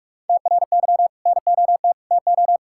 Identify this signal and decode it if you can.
Morse code, heard as a keyed tone, about 41 wpm, 700 Hz. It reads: TRY NOT TO
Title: Sentences from Top 1000 Words - Review Part 5 - 45wpm